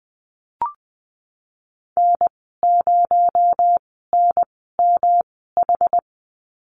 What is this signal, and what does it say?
Morse code, heard as a keyed tone, about 20 wpm, spaced as for 14 wpm, 700 Hz. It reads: N0NMH
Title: 245 Lesson 39g - Letter z - Call Signs 20wpm